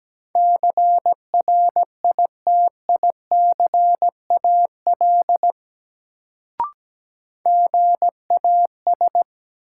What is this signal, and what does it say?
Morse code, heard as a keyed tone, about 17 wpm, 700 Hz. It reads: CRITICAL GAS
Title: Sentences from Top 1100 Words - New Word Drill - 17wpm